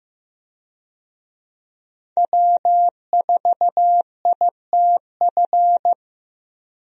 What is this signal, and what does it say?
Morse code, heard as a keyed tone, about 14 wpm, 700 Hz. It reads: W4ITF
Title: USA Call Signs - 5 Characters - Repeated 3x - 15wpm